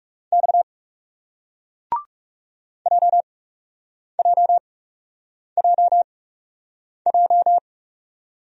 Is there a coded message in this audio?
Yes: Morse code from 0.3 to 7.6 s, heard as a keyed tone, about 42 wpm, spaced as for 8 wpm, 700 Hz. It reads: X JJJJ